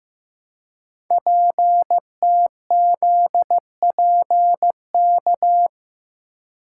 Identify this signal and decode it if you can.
Morse code, heard as a keyed tone, about 15 wpm, 700 Hz. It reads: PTZPK
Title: Code Groups - 5 Letters - 15wpm